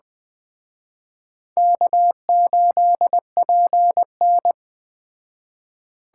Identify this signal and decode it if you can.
Morse code, heard as a keyed tone, about 20 wpm, 700 Hz. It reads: K8PN
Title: Speed Racing - USA 1x2 Call Signs 17wpm